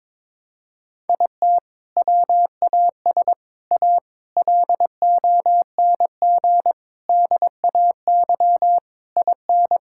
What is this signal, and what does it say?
Morse code, heard as a keyed tone, about 22 wpm, 700 Hz. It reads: IT WAS A LONG DAY IN
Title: Sentences from Top 1600 Words 22wpm